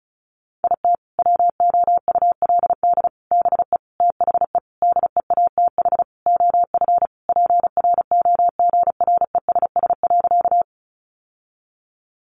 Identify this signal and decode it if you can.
Morse code, heard as a keyed tone, about 34 wpm, 700 Hz. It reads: IT WOULD BE THE DEATH OF PROGRESS.
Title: Sentences from Top 1400 Words 35wpm